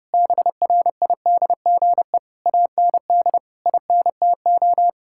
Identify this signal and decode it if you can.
Morse code, heard as a keyed tone, about 30 wpm, 700 Hz. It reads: BRIDGE AND INTO
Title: Sentences from Top 1600 Words 30wpm